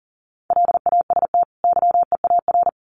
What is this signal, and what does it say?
Morse code, heard as a keyed tone, about 40 wpm, 700 Hz. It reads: LAST YEAR
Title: Sets of 2 Words 40wpm